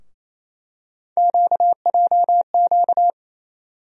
Morse code, heard as a keyed tone, about 28 wpm, 700 Hz. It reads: QJQ
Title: USA Call Sign Postfixes - 3 Letters 28wpm